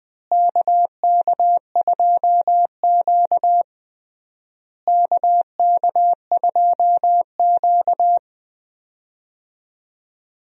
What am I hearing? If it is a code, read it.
Morse code, heard as a keyed tone, about 20 wpm, 700 Hz. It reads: KK2Q KK2Q
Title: USA Call Signs - 4 Characters - Repeated 3x - 20wpm